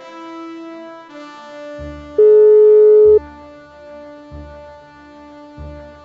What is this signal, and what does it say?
Ringback tone (Europe, 425 Hz).